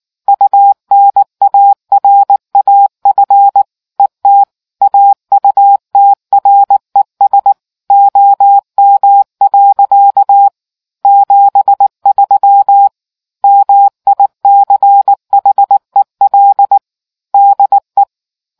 Morse code, heard as a keyed tone, about 19 wpm, 800 Hz. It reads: UNARAF ET AUTRES OM. 73 MICHEL DE